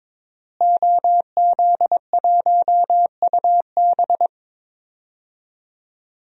Morse code, heard as a keyed tone, about 22 wpm, 700 Hz. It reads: OZ1UB
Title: DXCC-100 Call Signs - 5 Characters - Repeated 2x - 22wpm